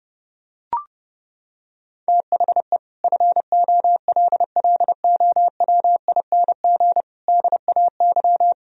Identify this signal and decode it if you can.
Morse code, heard as a keyed tone, about 30 wpm, 700 Hz. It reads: THE FOLLOWING DAY
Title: Sentences from Top 1000 Words - Review Part 4 - 30wpm